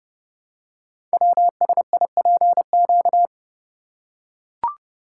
Morse code, heard as a keyed tone, about 30 wpm, 700 Hz. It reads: WSIPQ
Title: Code Groups - 5 Letters - 30wpm